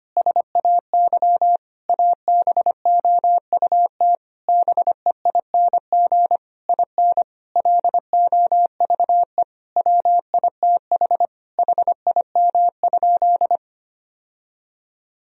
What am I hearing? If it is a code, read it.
Morse code, heard as a keyed tone, about 25 wpm, 700 Hz. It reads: SAY ABOUT BEING IN LOVE WITH HIM?
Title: Sentences from Top 1000 Words - Review Part 3 - 25wpm